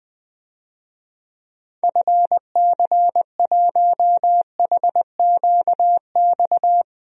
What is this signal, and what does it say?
Morse code, heard as a keyed tone, about 20 wpm, 700 Hz. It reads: FC1HQX
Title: DXCC-100 Call Signs - 6 Characters - 20wpm